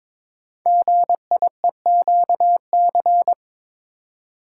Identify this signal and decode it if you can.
Morse code, heard as a keyed tone, about 22 wpm, 700 Hz. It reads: GIEQC
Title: Code Groups - 5 Letter:Numbers - 22wpm